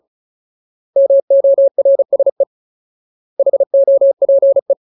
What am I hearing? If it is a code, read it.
Morse code, heard as a keyed tone, about 34 wpm, 550 Hz. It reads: MORSE HOPE
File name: Ultimate Word List / 35WPM Ultimate